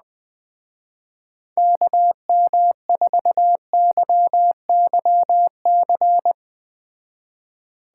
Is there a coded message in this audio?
Yes: Morse code from 1.6 to 6.3 s, heard as a keyed tone, about 20 wpm, 700 Hz. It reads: KM4YYC